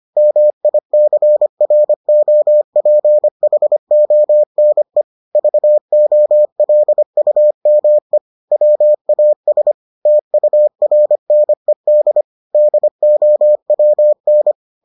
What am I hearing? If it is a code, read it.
Morse code, heard as a keyed tone, about 25 wpm, 600 Hz. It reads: MICROPHONE VOLUME WAS TURNED DOWN